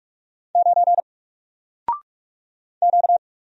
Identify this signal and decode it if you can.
Morse code, heard as a keyed tone, about 44 wpm, 700 Hz. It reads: 9 Q
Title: Single Letter-Number Alt 45wpm